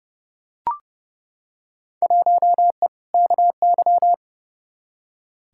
Morse code, heard as a keyed tone, about 30 wpm, 700 Hz. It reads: 1E KY
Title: Field Day Exchanges - 30wpm